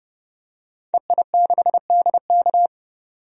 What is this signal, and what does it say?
Morse code, heard as a keyed tone, about 30 wpm, 700 Hz. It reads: EI6DK
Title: DXCC-100 Call Signs - 5 Characters - 30wpm